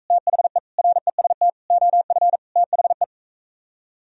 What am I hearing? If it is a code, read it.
Morse code, heard as a keyed tone, about 45 wpm, 700 Hz. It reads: THE REST OF THE